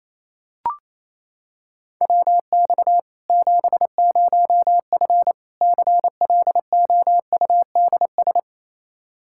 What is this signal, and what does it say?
Morse code, heard as a keyed tone, about 28 wpm, 700 Hz. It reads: WX 70F CLOUDS
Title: K1USN SST QSO Practice 28wpm